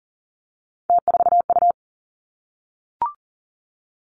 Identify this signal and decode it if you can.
Morse code, heard as a keyed tone, about 40 wpm, 700 Hz. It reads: T4U
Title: Three Challenging Letter-Number Combinations 40wpm